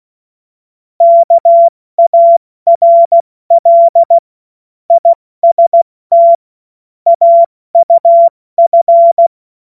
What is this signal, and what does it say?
Morse code, heard as a keyed tone, about 16 wpm, 680 Hz. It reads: KARL IST AUF
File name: Witz-02-September-2025-70-F80